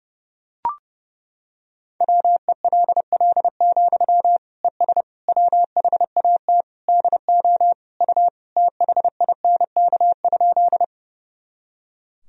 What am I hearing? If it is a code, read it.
Morse code, heard as a keyed tone, about 30 wpm, 700 Hz. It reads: WELL, ES WHAT DO U THINK?
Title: Sentences from Top 200 Words and Abbreviations 30wpm